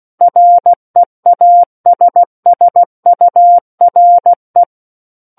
Morse code, heard as a keyed tone, about 16 wpm, 700 Hz. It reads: REASSURE
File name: Book One / 07-Chapter 7 - 16 WPM